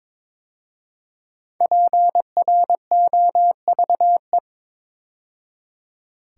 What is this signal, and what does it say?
Morse code, heard as a keyed tone, about 22 wpm, 700 Hz. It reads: PROVE
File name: 5 Letter Words - First 3 Letters Given - 22wpm